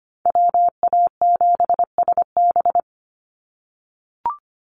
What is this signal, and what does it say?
Morse code, heard as a keyed tone, about 25 wpm, 700 Hz. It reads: WA7SB
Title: NA SOTA Chaser Callsigns 25wpm